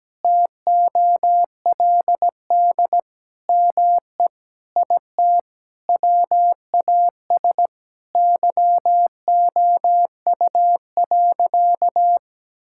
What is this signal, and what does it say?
Morse code, heard as a keyed tone, about 17 wpm, 700 Hz. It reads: TOLD ME IT WAS YOU.
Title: Sentences from Top 1600 Words 17wpm